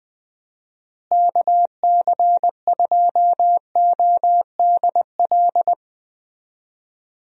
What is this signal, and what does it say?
Morse code, heard as a keyed tone, about 20 wpm, 700 Hz. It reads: KC2ODL
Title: USA Call Signs - 6 Characters - Repeated 2x - 20wpm